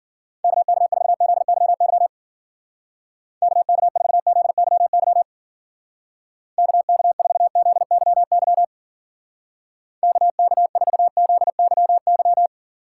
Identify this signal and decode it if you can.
Morse code, heard as a keyed tone, about 47 wpm, 700 Hz. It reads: KK4ZYY KK4ZYY KK4ZYY KK4ZYY
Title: Speed Racing - USA 2x3 Call Signs 40wpm